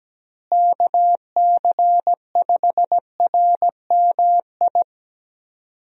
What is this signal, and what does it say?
Morse code, heard as a keyed tone, about 17 wpm, 700 Hz. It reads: KC5RMI
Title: USA Call Signs - 6 Characters - Repeated 3x - 17wpm